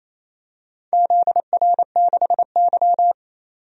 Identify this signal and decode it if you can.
Morse code, heard as a keyed tone, about 28 wpm, 700 Hz. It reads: ZR6Y
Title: DXCC-100 Call Signs - 4 Characters - 28wpm